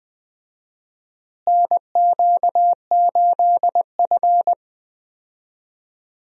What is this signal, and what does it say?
Morse code, heard as a keyed tone, about 20 wpm, 700 Hz. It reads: NQ8F
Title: USA Call Signs - 4 Characters - Repeated 3x - 20wpm